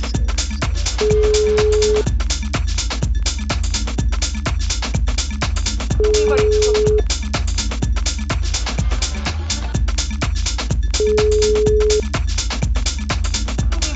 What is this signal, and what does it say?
Ringback tone (Europe, 425 Hz), with speech.